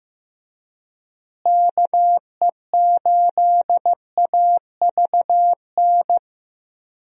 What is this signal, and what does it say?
Morse code, heard as a keyed tone, about 15 wpm, 700 Hz. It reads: KE8AVN
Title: USA Call Signs - 6 Characters - Repeated 3x - 15wpm